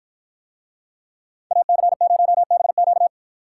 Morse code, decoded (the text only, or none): AC0BQ